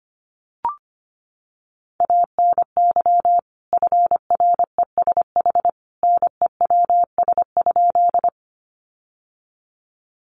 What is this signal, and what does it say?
Morse code, heard as a keyed tone, about 25 wpm, 700 Hz. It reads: ANY FRESH NEWS?
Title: Sentences from Top 1200 Words 25wpm